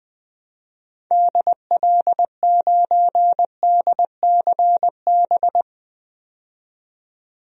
Morse code, heard as a keyed tone, about 21 wpm, 700 Hz. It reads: DL9DCB
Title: DXCC-100 Call Signs - 6 Characters - 20wpm